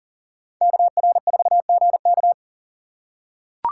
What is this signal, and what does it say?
Morse code, heard as a keyed tone, about 40 wpm, 700 Hz. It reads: KR4GK
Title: USA Call Signs - 5 Characters - 3x Character Spacing - 40wpm